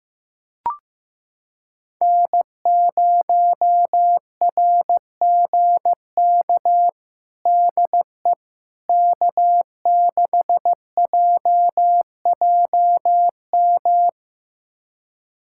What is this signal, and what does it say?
Morse code, heard as a keyed tone, about 15 wpm, 700 Hz. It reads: N0RGK DE K6JJM
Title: USA Callsign DE Callsign - 5 Characters - 15wpm